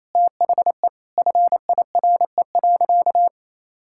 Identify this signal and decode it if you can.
Morse code, heard as a keyed tone, about 27 wpm, 700 Hz. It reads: THE FIRE.
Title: Sentences from Top 1600 Words 28wpm